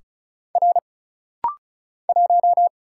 Morse code, heard as a keyed tone, about 36 wpm, 700 Hz. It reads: R 1
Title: Single Letter-Number Alt - Rapid-Fire 35wpm